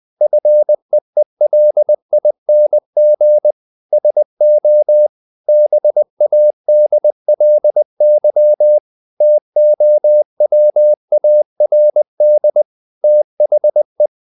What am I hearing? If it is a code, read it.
Morse code, heard as a keyed tone, about 20 wpm, 600 Hz. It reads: FEELING SO BADLY TOWARD THE
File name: the old man and the sea part 2 20wpm